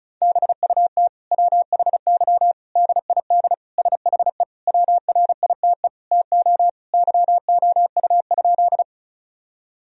Morse code, heard as a keyed tone, about 34 wpm, 700 Hz. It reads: BUT WHY DID SHE WRITE TO YOU?
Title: Sentences from Top 1000 Words - Review Part 1 - 35wpm